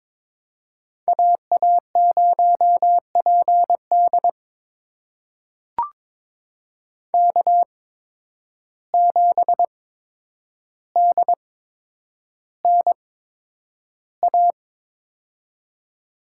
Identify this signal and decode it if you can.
Morse code, heard as a keyed tone, about 22 wpm, spaced as for 9 wpm, 700 Hz. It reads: AA0PD K7DNA
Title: USA Call Signs - 5 Characters - 8x Character Spacing - 22wpm